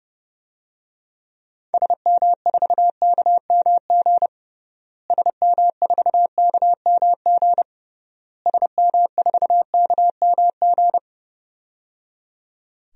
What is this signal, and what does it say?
Morse code, heard as a keyed tone, about 30 wpm, 700 Hz. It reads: SM4KMG SM4KMG SM4KMG
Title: DXCC-100 Call Signs - 6 Characters - Repeated 3x - 30wpm